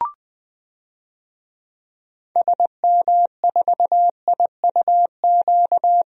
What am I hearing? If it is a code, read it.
Morse code, heard as a keyed tone, about 21 wpm, 700 Hz. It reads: SM4IUQ
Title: DXCC-100 Call Signs - 6 Characters - 20wpm